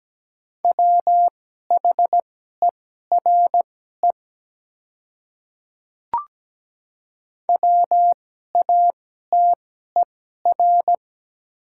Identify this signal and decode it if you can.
Morse code, heard as a keyed tone, about 17 wpm, spaced as for 12 wpm, 700 Hz. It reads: WHERE WATER
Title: 2x Character Spacing - Top 200 Words - 17wpm